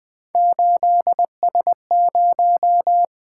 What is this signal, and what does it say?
Morse code, heard as a keyed tone, about 20 wpm, 700 Hz. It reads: 8S0
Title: Three Letter-Number Combinations - 20wpm